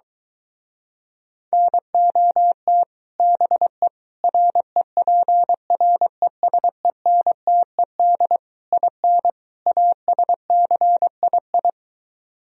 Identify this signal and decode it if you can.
Morse code, heard as a keyed tone, about 24 wpm, 700 Hz. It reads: NOT BE REPRESENTED IN ASCII